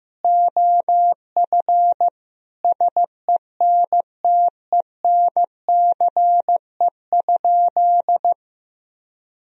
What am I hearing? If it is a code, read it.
Morse code, heard as a keyed tone, about 15 wpm, 700 Hz. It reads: OF SENTENCE?
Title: Sentences from Top 1000 Words - Review Part 1 - 15wpm